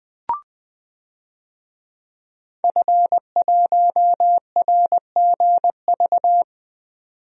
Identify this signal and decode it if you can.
Morse code, heard as a keyed tone, about 20 wpm, 700 Hz. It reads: F1RGV
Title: DXCC-100 Call Signs - 5 Characters - 20wpm